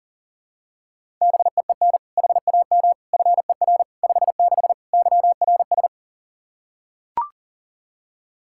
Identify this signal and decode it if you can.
Morse code, heard as a keyed tone, about 40 wpm, 700 Hz. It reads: BEEN HAM FER 56 YRS
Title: Practice QSO - Exchange 3 - More Info - 40wpm